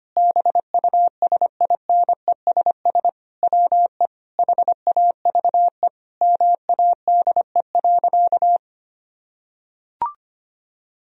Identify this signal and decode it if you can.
Morse code, heard as a keyed tone, about 25 wpm, 700 Hz. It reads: BUSINESS WE HAVE MADE.
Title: Sentences from Top 1000 Words - Review Part 2 - 25wpm